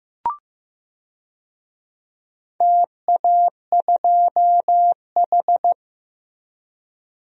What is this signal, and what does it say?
Morse code, heard as a keyed tone, about 15 wpm, 700 Hz. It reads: TA2H
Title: DXCC-100 Call Signs - 4 Characters - Repeated 2x - 15wpm